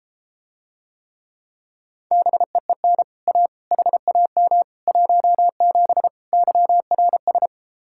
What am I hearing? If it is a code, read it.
Morse code, heard as a keyed tone, about 34 wpm, 700 Hz. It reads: BEEN A HAM 17 YRS